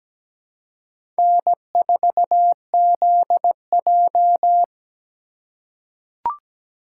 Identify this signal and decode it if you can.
Morse code, heard as a keyed tone, about 17 wpm, 700 Hz. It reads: N4ZJ
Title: USA Call Signs - 4 Characters - 3x Character Spacing - 17wpm